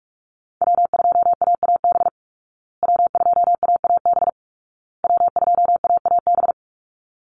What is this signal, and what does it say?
Morse code, heard as a keyed tone, about 44 wpm, 700 Hz. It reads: W2AAB W2AAB W2AAB
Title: Call Signs Repeated 2x - 45wpm